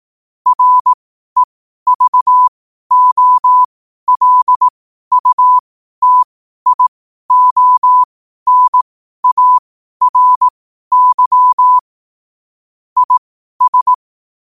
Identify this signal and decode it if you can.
Morse code, heard as a keyed tone, about 18 wpm, spaced as for 12 wpm, 1000 Hz. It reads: REVOLUTIONARY IS
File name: qotdfiles / 12wpm